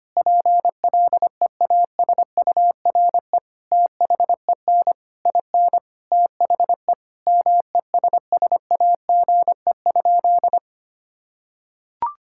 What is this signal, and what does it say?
Morse code, heard as a keyed tone, about 25 wpm, 700 Hz. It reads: PLEASURE THEN IN THE MESSAGE?
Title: Sentences from Top 1600 Words 25wpm